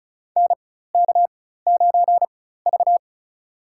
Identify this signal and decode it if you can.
Morse code, heard as a keyed tone, about 34 wpm, spaced as for 16 wpm, 700 Hz. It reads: NK9V
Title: USA Call Signs - 4 Characters - 4x Character Spacing - 35wpm